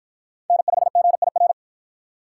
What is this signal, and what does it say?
Morse code, heard as a keyed tone, about 48 wpm, 700 Hz. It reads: N5GIR